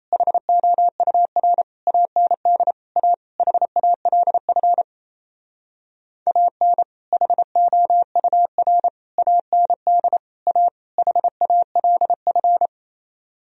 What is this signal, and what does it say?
Morse code, heard as a keyed tone, about 34 wpm, 700 Hz. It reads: HOUR AND A HALF AN HOUR AND A HALF
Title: Speed-Racing - Sets of 5 Words 28wpm